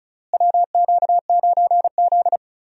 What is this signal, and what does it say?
Morse code, heard as a keyed tone, about 34 wpm, 700 Hz. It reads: WQ9Z
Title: USA Call Signs - 4 Characters - Repeated 3x - 35wpm